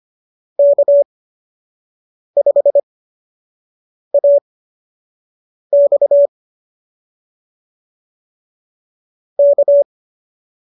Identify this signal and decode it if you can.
Morse code, heard as a keyed tone, about 25 wpm, spaced as for 6 wpm, 590 Hz. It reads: K5AX K